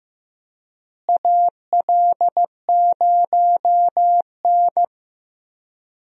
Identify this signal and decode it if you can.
Morse code, heard as a keyed tone, about 15 wpm, 700 Hz. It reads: AL0N
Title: USA Call Signs - 4 Characters - Repeated 3x - 15wpm